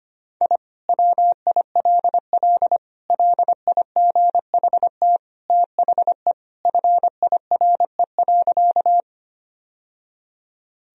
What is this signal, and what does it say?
Morse code, heard as a keyed tone, about 25 wpm, 700 Hz. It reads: I WILL LIGHT THE FIRE.
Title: Sentences from Top 1000 Words - Review Part 1 - 25wpm